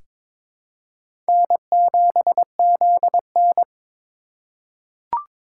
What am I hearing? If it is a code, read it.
Morse code, heard as a keyed tone, about 22 wpm, 700 Hz. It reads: N7ZN
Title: K1USN SST QSO Practice - 22wpm